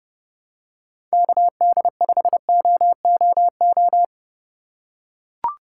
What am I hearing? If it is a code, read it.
Morse code, heard as a keyed tone, about 30 wpm, 700 Hz. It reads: KD5OOO
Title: USA Call Signs - 6 Characters - 4x Character Spacing - 30wpm